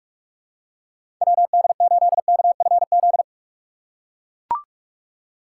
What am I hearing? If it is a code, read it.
Morse code, heard as a keyed tone, about 42 wpm, 700 Hz. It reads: WD8KFZ
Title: USA Call Signs - 3x Character Spacing - 45wpm